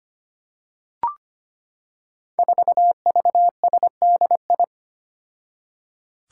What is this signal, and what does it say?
Morse code, heard as a keyed tone, about 25 wpm, 700 Hz. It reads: 4VSDI
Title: Five Challenging Letter-Number Combinations 25wpm